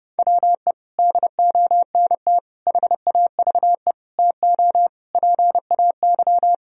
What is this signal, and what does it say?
Morse code, heard as a keyed tone, about 32 wpm, 700 Hz. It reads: WE DONT HAVE TO PAY
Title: Sentences from Top 1000 Words - Review Part 2 - 30wpm